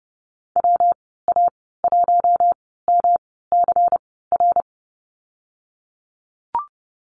Morse code, heard as a keyed tone, about 30 wpm, spaced as for 17 wpm, 700 Hz. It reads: WA1MCR
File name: Lesson 19g - Letter w - Call Signs 30wpm